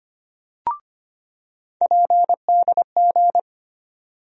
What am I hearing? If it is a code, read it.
Morse code, heard as a keyed tone, about 25 wpm, 700 Hz. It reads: PDG
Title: USA Call Sign Postfixes - 3 Letters 25wpm